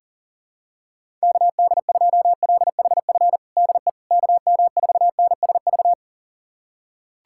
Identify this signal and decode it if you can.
Morse code, heard as a keyed tone, about 40 wpm, 700 Hz. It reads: KD2LHF DE KM4NSV
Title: USA Callsign DE Callsign - 6 Characters - 40wpm